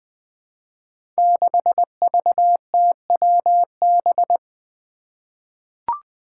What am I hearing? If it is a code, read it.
Morse code, heard as a keyed tone, about 20 wpm, 700 Hz. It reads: 6VTWB
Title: Five Challenging Letter-Number Combinations 20wpm